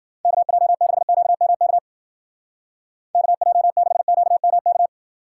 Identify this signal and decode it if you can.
Morse code, heard as a keyed tone, about 48 wpm, 700 Hz. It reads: KJ6QMK KJ6QMK
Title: Speed Racing - USA 2x3 Call Signs 40wpm